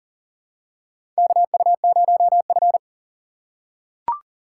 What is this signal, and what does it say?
Morse code, heard as a keyed tone, about 40 wpm, 700 Hz. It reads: KU0F
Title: USA Call Signs - 4 Characters - 3x Character Spacing - 40wpm